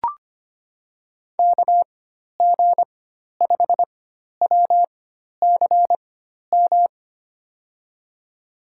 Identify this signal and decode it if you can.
Morse code, heard as a keyed tone, about 25 wpm, spaced as for 12 wpm, 700 Hz. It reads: KG5WCM
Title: USA Call Signs - 6 Characters - 4x Character Spacing - 25wpm